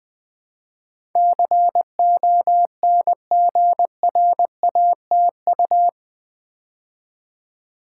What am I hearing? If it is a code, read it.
Morse code, heard as a keyed tone, about 20 wpm, 700 Hz. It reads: CONGRATU
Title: Build A Word - 4 Letters At A Time - 12-Letter Words - 20wpm